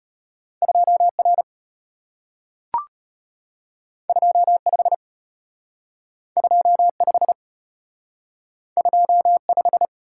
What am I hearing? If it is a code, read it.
Morse code, heard as a keyed tone, about 34 wpm, 700 Hz. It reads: 2R 25 25 25